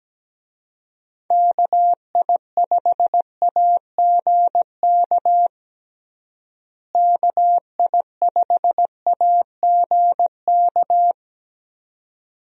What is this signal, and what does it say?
Morse code, heard as a keyed tone, about 17 wpm, 700 Hz. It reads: KI5AGK KI5AGK
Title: USA Call Signs - 6 Characters - Repeated 2x - 17wpm